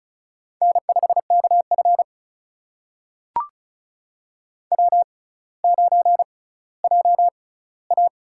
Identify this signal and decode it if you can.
Morse code, heard as a keyed tone, about 36 wpm, spaced as for 18 wpm, 700 Hz. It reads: N5KF W9JA
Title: USA Call Signs - 4 Characters - 6x Character Spacing - 35wpm